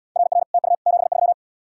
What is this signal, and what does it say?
Morse code, heard as a keyed tone, about 71 wpm, 690 Hz. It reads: RR TU 73